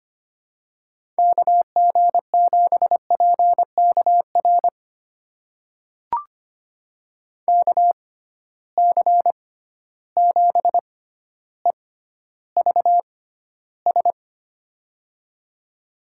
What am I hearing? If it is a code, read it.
Morse code, heard as a keyed tone, about 25 wpm, spaced as for 13 wpm, 700 Hz. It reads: KG7PKR KC7EVS
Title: USA Call Signs - 6 Characters - 6x Character Spacing - 25wpm